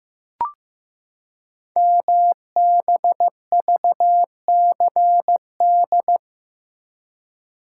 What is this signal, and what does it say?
Morse code, heard as a keyed tone, about 15 wpm, 700 Hz. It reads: MBVCD
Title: Code Groups - 5 Letters - 15wpm